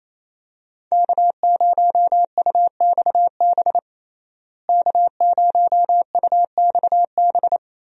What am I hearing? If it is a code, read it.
Morse code, heard as a keyed tone, about 27 wpm, 700 Hz. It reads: K0UXB K0UXB
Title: USA Call Signs - 5 Characters - Repeated 2x - 28wpm